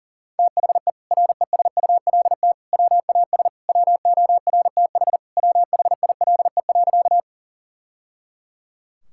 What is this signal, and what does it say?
Morse code, heard as a keyed tone, about 40 wpm, 700 Hz. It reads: THE RESULT WAS WORTH WHILE.
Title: Sentences from Top 1000 Words - Review Part 5 - 40wpm